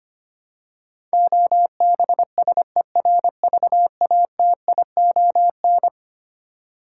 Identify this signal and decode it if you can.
Morse code, heard as a keyed tone, about 25 wpm, 700 Hz. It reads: OBSERVATION
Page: morse-code-ninja-practice-25wpm